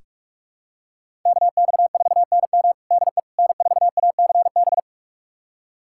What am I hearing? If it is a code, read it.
Morse code, heard as a keyed tone, about 46 wpm, 700 Hz. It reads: KX4NM DE N4ACB
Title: USA Callsign DE Callsign 45wpm